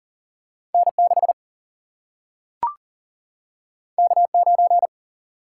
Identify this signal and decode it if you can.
Morse code, heard as a keyed tone, about 39 wpm, 700 Hz. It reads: N 6 K 9
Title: Single-Letter Single-Number 40wpm